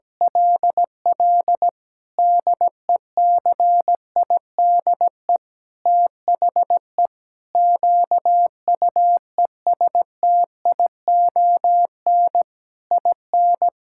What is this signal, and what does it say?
Morse code, heard as a keyed tone, about 17 wpm, 700 Hz. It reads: LL DECIDE THE QUESTION IN
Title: Sentences from Top 1600 Words 17wpm